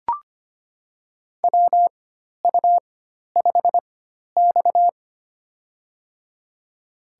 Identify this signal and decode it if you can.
Morse code, heard as a keyed tone, about 25 wpm, spaced as for 12 wpm, 700 Hz. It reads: WU5X
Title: USA Call Signs - 4x Character Spacing - 25wpm